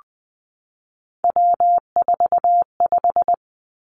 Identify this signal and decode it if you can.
Morse code, heard as a keyed tone, about 20 wpm, 700 Hz. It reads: W45
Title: Three Challenging Letter-Number Combinations 20wpm